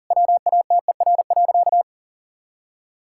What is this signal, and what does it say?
Morse code, heard as a keyed tone, about 40 wpm, 700 Hz. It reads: WATER.
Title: Sentences from Top 1600 Words 40wpm